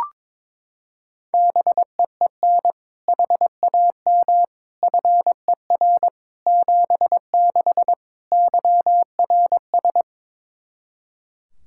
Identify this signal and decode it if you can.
Morse code, heard as a keyed tone, about 22 wpm, 700 Hz. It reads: BEEN HAM FER 76 YRS
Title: Practice QSO - Exchange 3 - More Info - 22wpm